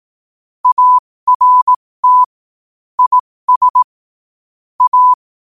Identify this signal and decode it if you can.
Morse code, heard as a keyed tone, about 18 wpm, spaced as for 15 wpm, 1000 Hz. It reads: ART IS A
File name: qotdfiles / 15wpm